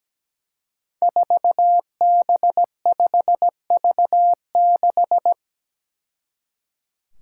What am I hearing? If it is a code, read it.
Morse code, heard as a keyed tone, about 17 wpm, 700 Hz. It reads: 4B5V6
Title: Five Challenging Letter-Number Combinations 17wpm